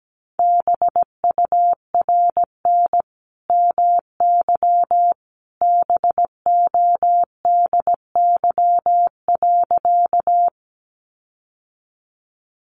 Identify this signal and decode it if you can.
Morse code, heard as a keyed tone, about 17 wpm, 700 Hz. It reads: BURN MY BODY.
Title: Sentences from Top 1300 Words 17wpm